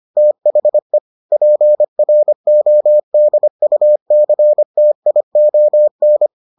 Morse code, heard as a keyed tone, about 25 wpm, 600 Hz. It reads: THE PRODUCTION